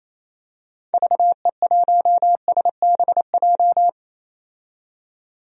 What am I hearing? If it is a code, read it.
Morse code, heard as a keyed tone, about 28 wpm, 700 Hz. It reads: VE1SBJ
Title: DXCC-100 Call Signs - 6 Characters - 28wpm